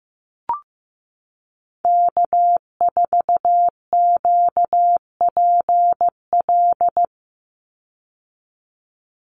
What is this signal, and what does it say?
Morse code, heard as a keyed tone, about 15 wpm, 700 Hz. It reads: K4QPL
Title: Call Signs Repeated 2x - 15wpm